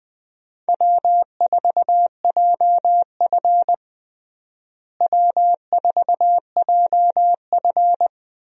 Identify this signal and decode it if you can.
Morse code, heard as a keyed tone, about 20 wpm, 700 Hz. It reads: W4JF W4JF
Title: USA Call Signs - 4 Characters - Repeated 3x - 20wpm